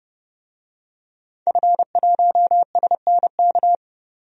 Morse code, heard as a keyed tone, about 30 wpm, 700 Hz. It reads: F1SNK
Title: DXCC-100 Call Signs - 5 Characters - 30wpm